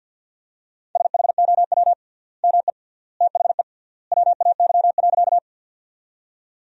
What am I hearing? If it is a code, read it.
Morse code, heard as a keyed tone, about 47 wpm, 700 Hz. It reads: SHOW ME THE WAY.
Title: 3x Word-Spacing - Sentences from Top 200 Words - 50wpm